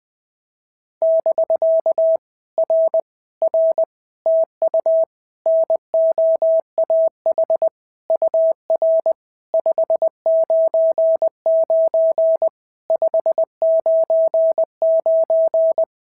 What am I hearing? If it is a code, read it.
Morse code, heard as a keyed tone, about 20 wpm, 650 Hz. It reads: <BK> R R TU NOAH UR 599 599
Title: POTA QSO Practice - 20wpm